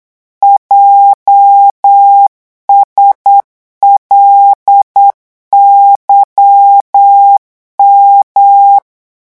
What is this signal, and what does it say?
Morse code, heard as a keyed tone, about 8 wpm, 800 Hz. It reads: JSLYM
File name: morse mp3 / revision8-vitesse 8 mots